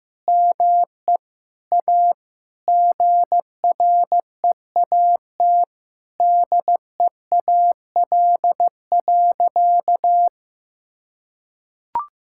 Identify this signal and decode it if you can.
Morse code, heard as a keyed tone, about 15 wpm, 700 Hz. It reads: ME A GREAT DEAL.
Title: Sentences from Top 1000 Words - Review Part 6 - 15wpm